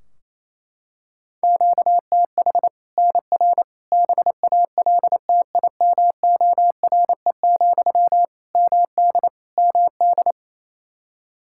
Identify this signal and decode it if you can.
Morse code, heard as a keyed tone, about 28 wpm, 700 Hz. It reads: QTH NR BALTIMORE, MD MD
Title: Practice QSO - Exchange 1 - Beginning - 28wpm